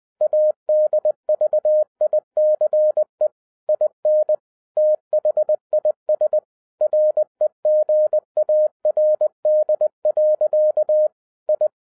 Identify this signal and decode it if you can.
Morse code, heard as a keyed tone, about 20 wpm, 610 Hz. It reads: ADVICE IN THIS REGARD. I